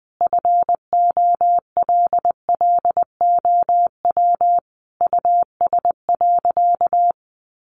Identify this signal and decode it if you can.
Morse code, heard as a keyed tone, about 20 wpm, 700 Hz. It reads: FOLLOW US.
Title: Sentences from Top 1500 Words 20wpm